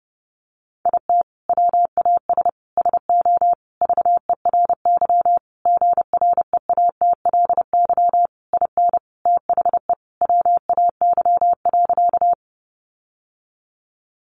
Morse code, heard as a keyed tone, about 28 wpm, 700 Hz. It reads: IT WAS SO VERY GREATLY IN THE WAY.
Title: Sentences from Top 1400 Words 30wpm